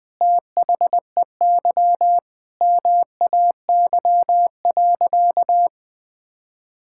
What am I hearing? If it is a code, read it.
Morse code, heard as a keyed tone, about 20 wpm, 700 Hz. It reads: THEY MAY.
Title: Sentences with 8-Word Phrases - 20wpm